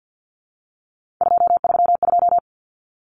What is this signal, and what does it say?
Morse code, heard as a keyed tone, about 50 wpm, 700 Hz. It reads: 232
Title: Numbers - 3-Digit 50wpm